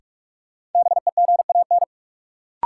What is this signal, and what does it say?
Morse code, heard as a keyed tone, about 46 wpm, 700 Hz. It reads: BEGAN
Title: Sentences from Top 300 Words - New Word Drill - 45wpm